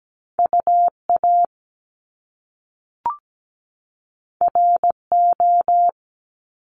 Morse code, heard as a keyed tone, about 17 wpm, 700 Hz. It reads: UA RO
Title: Top Two Letter Combinations in Words 17wpm